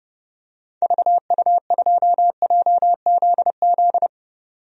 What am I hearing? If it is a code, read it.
Morse code, heard as a keyed tone, about 30 wpm, 700 Hz. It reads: VU2JZZ
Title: DXCC-100 Call Signs - 6 Characters - 30wpm